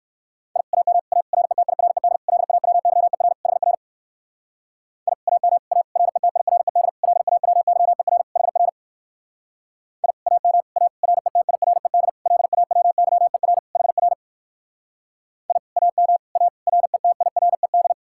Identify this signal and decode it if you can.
Morse code, heard as a keyed tone, about 58 wpm, 700 Hz. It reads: I AM A RETIRED LAWYER HR I AM A RETIRED LAWYER HR I AM A RETIRED LAWYER HR I AM A RETIRED